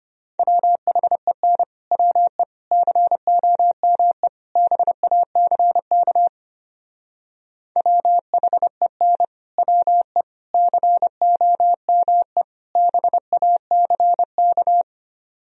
Morse code, heard as a keyed tone, about 29 wpm, 700 Hz. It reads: WHEN WE COME BACK WHEN WE COME BACK
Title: Speed-Racing - Sets of 4 Words 25wpm